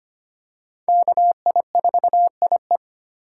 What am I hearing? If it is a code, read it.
Morse code, heard as a keyed tone, about 25 wpm, 700 Hz. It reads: KI4IE
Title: USA Call Signs - 5 Characters - 2x Character Spacing - 25wpm